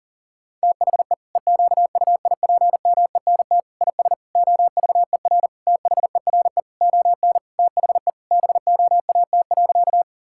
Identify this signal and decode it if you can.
Morse code, heard as a keyed tone, about 40 wpm, 700 Hz. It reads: THE EQUIPMENT IS OVER THERE ON THE BOAT.
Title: Sentences from Top 1000 Words - Review Part 5 - 40wpm